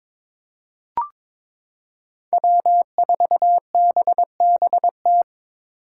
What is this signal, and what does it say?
Morse code, heard as a keyed tone, about 22 wpm, 700 Hz. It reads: W4BBT
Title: K1USN SST QSO Practice - 22wpm